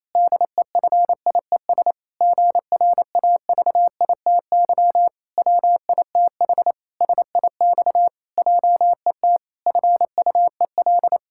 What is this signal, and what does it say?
Morse code, heard as a keyed tone, about 28 wpm, 700 Hz. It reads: DEFIES GRAVITY WITH SIX JET FUEL
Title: Pangram Challenge v2 - 28wpm